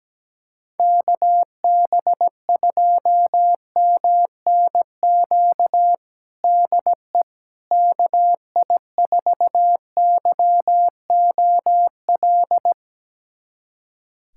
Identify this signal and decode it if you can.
Morse code, heard as a keyed tone, about 17 wpm, 700 Hz. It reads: KB2MNQ DE KI4YOL
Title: USA Callsign DE Callsign - 6 Characters - 17wpm